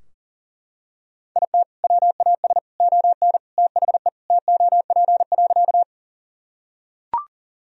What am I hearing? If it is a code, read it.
Morse code, heard as a keyed tone, about 37 wpm, 700 Hz. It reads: IT WAS ON THE TOP.
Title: Sentences from Top 1000 Words - Review Part 2 - 40wpm